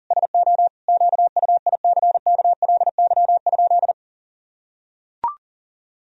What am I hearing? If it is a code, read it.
Morse code, heard as a keyed tone, about 37 wpm, 700 Hz. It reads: SO QUICKLY?
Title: Sentences from Top 1000 Words - Review Part 3 - 40wpm